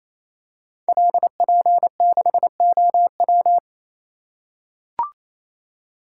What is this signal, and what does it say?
Morse code, heard as a keyed tone, about 29 wpm, 700 Hz. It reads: LP6OW
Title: Code Groups - 5 Letter:Numbers - 28wpm